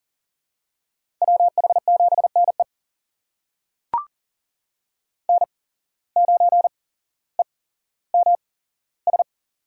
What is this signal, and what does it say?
Morse code, heard as a keyed tone, about 40 wpm, spaced as for 17 wpm, 700 Hz. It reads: WH7NE N9EMS
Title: USA Call Signs - 5 Characters - 8x Character Spacing - 40wpm